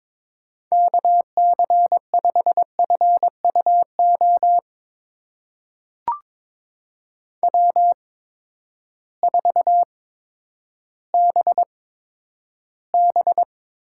Morse code, heard as a keyed tone, about 22 wpm, spaced as for 11 wpm, 700 Hz. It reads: KC5FUO W4BB
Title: USA Call Signs - 8x Character Spacing - 22wpm